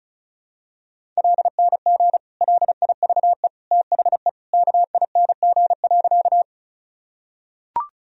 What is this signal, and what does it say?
Morse code, heard as a keyed tone, about 36 wpm, 700 Hz. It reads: LNG LIVE THE KING.
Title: Sentences from Top 400 Words and Abbreviations 35wpm